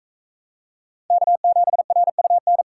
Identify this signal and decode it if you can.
Morse code, heard as a keyed tone, about 41 wpm, 700 Hz. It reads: K7RUN